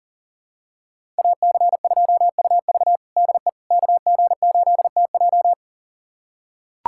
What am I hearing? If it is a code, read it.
Morse code, heard as a keyed tone, about 40 wpm, 700 Hz. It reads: AC2UV DE KG8TJ
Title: USA Callsign DE Callsign - 5 Characters - 40wpm